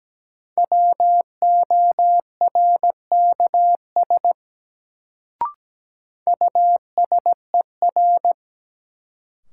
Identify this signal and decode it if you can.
Morse code, heard as a keyed tone, about 17 wpm, 700 Hz. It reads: WORKS USER
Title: Sentences from Top 1000 Words - Review Part 5 - New Word Drill - 17wpm